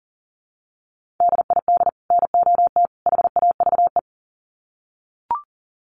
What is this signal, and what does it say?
Morse code, heard as a keyed tone, about 40 wpm, 700 Hz. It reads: DID NOT HAVE
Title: Sets of 3 Words 40wpm